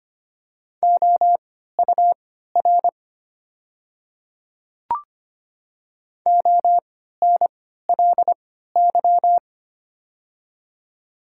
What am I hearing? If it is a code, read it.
Morse code, heard as a keyed tone, about 25 wpm, spaced as for 14 wpm, 700 Hz. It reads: OUR ONLY
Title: 3x Character Spacing - Top 200 Words - 25wpm